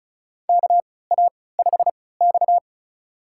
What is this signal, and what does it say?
Morse code, heard as a keyed tone, about 36 wpm, spaced as for 20 wpm, 700 Hz. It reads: KA5X
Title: USA Call Signs - 4 Characters - 3x Character Spacing - 35wpm